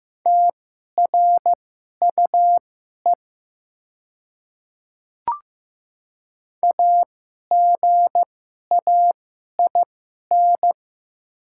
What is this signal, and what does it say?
Morse code, heard as a keyed tone, about 15 wpm, spaced as for 11 wpm, 700 Hz. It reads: TRUE AGAIN
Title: 2x Character Spacing - Top 500 Words - 15wpm